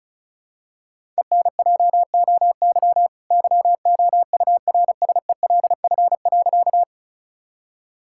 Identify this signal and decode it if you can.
Morse code, heard as a keyed tone, about 36 wpm, 700 Hz. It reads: ENJOY YOURSELF.
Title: Sentences from Top 1000 Words - Review Part 5 - 35wpm